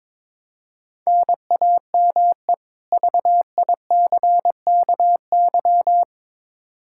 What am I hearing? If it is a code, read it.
Morse code, heard as a keyed tone, about 22 wpm, 700 Hz. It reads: NAME VICKY
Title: Practice QSO - Exchange 1 - Beginning - 22wpm